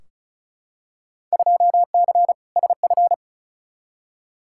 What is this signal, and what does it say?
Morse code, heard as a keyed tone, about 36 wpm, 690 Hz. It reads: 2C SF
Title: Field Day Exchanges Encore - 35wpm